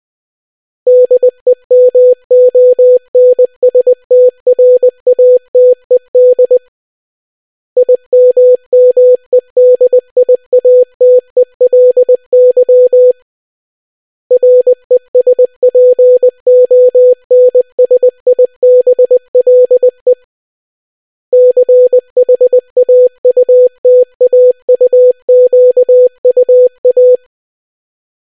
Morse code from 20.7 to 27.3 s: CHAUTAUQUA